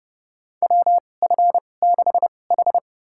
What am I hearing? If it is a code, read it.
Morse code, heard as a keyed tone, about 31 wpm, spaced as for 22 wpm, 700 Hz. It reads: WF6H
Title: USA Call Signs - 4 Characters - 2x Character Spacing - 30wpm